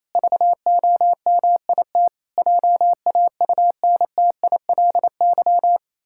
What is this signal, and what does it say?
Morse code, heard as a keyed tone, about 28 wpm, 700 Hz. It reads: VOMIT JAUNTILY
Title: Pangram Challenge v2 - 28wpm